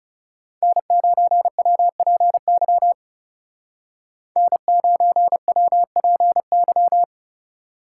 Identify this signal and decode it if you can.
Morse code, heard as a keyed tone, about 37 wpm, 700 Hz. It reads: N9WPY N9WPY
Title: Speed Racing - USA Call Signs 30wpm